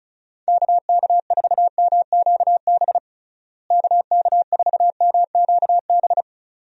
Morse code, heard as a keyed tone, about 36 wpm, 700 Hz. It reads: KK4MQB KK4MQB
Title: USA Call Signs - 6 Characters - Repeated 2x - 35wpm